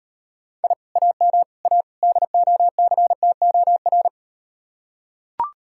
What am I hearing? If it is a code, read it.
Morse code, heard as a keyed tone, about 37 wpm, 700 Hz. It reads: I AM A DOCTOR